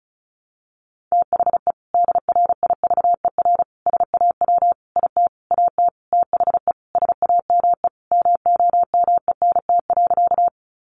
Morse code, heard as a keyed tone, about 34 wpm, 700 Hz. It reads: THE DRIVER SAW IT AT THE SAME MOMENT.
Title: Sentences from Top 1000 Words 35wpm